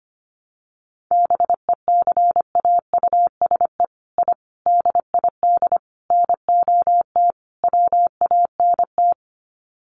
Morse code, heard as a keyed tone, about 25 wpm, 700 Hz. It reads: BECAUSE I DID NOT WANT
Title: Sets of 5 Words Encore2 25wpm